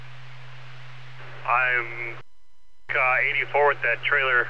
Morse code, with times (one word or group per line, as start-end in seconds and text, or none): none